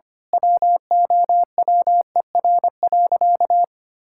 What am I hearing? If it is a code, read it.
Morse code, heard as a keyed tone, about 25 wpm, 700 Hz. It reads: WOWER.
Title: Pangram Challenge v2 - 25wpm